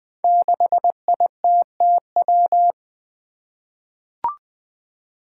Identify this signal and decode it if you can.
Morse code, heard as a keyed tone, about 20 wpm, 700 Hz. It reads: 6ITTW
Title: Five Challenging Letter-Number Combinations 20wpm